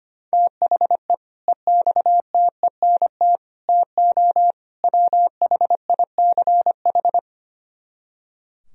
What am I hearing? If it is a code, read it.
Morse code, heard as a keyed tone, about 32 wpm, 700 Hz. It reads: THE EXTENT TO WHICH